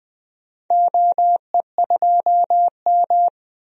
Morse code, heard as a keyed tone, about 20 wpm, 700 Hz. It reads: OE2M
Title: DXCC-100 Call Signs - 4 Characters - Repeated 3x - 20wpm